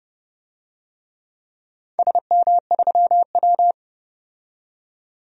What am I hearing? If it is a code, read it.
Morse code, heard as a keyed tone, about 30 wpm, 700 Hz. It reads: SM3W
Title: DXCC-100 Call Signs - 4 Characters - 30wpm